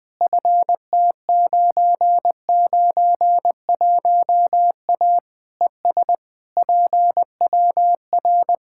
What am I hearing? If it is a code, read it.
Morse code, heard as a keyed tone, about 20 wpm, 700 Hz. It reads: FT991A ES PWR
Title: Practice QSO - Exchange 2 - Info - 20wpm